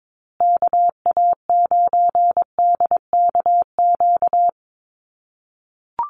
Morse code, heard as a keyed tone, about 22 wpm, 700 Hz. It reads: KA9DKQ
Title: USA Call Signs Repeated 3x 22wpm